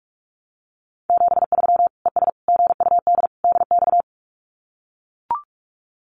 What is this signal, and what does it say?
Morse code, heard as a keyed tone, about 44 wpm, 700 Hz. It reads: 73 ES GUD DX
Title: Common QSO Phrases 45wpm